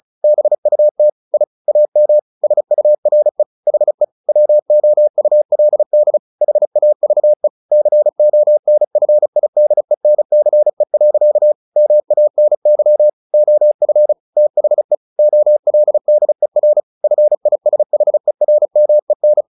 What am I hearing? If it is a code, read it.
Morse code, heard as a keyed tone, about 34 wpm, 600 Hz. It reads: BUT I AM SURE HE WOULD HAVE CONFIDENCE. MANY OF THE OLDER FISHERMEN
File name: the old man and the sea 2 35wpm